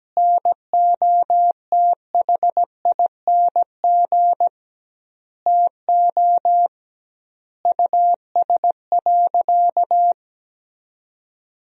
Morse code, heard as a keyed tone, about 17 wpm, 700 Hz. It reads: NOTHING TO US.
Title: 2x Word-Spacing - Sentences from Top 500 Words - 17wpm